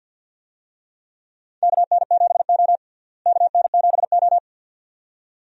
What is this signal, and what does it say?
Morse code, heard as a keyed tone, about 52 wpm, 700 Hz. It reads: KN7O KN7O
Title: USA Call Signs - 4 Characters - Repeated 2x - 50wpm